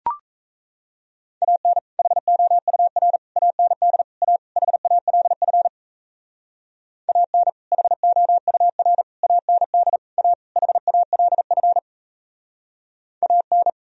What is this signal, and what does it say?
Morse code, heard as a keyed tone, about 34 wpm, 700 Hz. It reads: AN HOUR AND A HALF AN HOUR AND A HALF AN
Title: Speed-Racing - Sets of 5 Words 28wpm